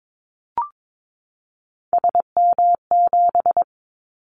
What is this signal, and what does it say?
Morse code, heard as a keyed tone, about 22 wpm, 700 Hz. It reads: SM7
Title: Callsign Prefixes for Top 25 DXCC - 22wpm